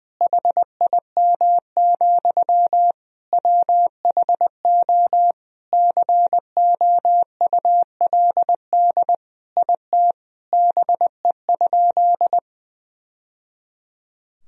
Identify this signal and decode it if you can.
Morse code, heard as a keyed tone, about 20 wpm, 700 Hz. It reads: HIM, WHO COULD IT BE?
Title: Sentences from Top 1000 Words - Review Part 2 - 20wpm